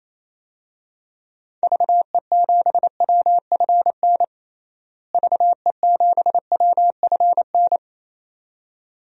Morse code, heard as a keyed tone, about 27 wpm, 700 Hz. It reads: VE7WFN VE7WFN
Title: DXCC-100 Call Signs - 6 Characters - Repeated 2x - 28wpm